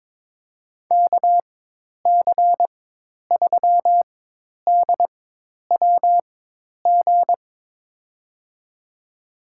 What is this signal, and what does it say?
Morse code, heard as a keyed tone, about 22 wpm, spaced as for 10 wpm, 700 Hz. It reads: KC3DWG
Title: USA Call Signs - 4x Character Spacing - 22wpm